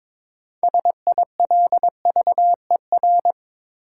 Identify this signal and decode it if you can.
Morse code, heard as a keyed tone, about 21 wpm, 700 Hz. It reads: SILVER
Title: Six Letter Words - 22wpm